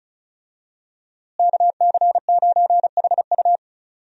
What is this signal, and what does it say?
Morse code, heard as a keyed tone, about 36 wpm, 700 Hz. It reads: KC9HU
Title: USA Call Signs - 5 Characters - Repeated 3x - 35wpm